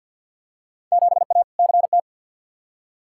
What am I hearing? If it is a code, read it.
Morse code, heard as a keyed tone, about 49 wpm, 700 Hz. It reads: 7A CT
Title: Field Day Exchanges - 50wpm